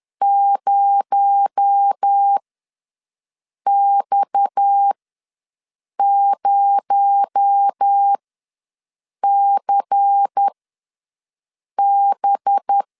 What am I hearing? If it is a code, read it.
Morse code, heard as a keyed tone, about 10 wpm, spaced as for 5 wpm, 800 Hz. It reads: 0X0CB
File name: LEZIONI / 6 PKX0 / 6 PKX0 ws 10wpm 4220